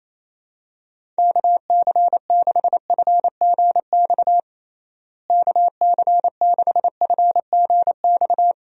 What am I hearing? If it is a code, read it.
Morse code, heard as a keyed tone, about 28 wpm, 700 Hz. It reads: KC6FGX KC6FGX
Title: USA Call Signs - 6 Characters - Repeated 3x - 28wpm